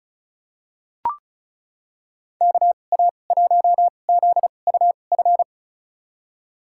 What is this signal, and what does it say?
Morse code, heard as a keyed tone, about 34 wpm, spaced as for 25 wpm, 700 Hz. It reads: KA1ZUF
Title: USA Call Signs - 6 Characters - 2x Character Spacing - 35wpm